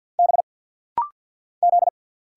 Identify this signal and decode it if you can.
Morse code, heard as a keyed tone, about 50 wpm, 700 Hz. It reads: B Z
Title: Single Letters - Rapid-Fire 50wpm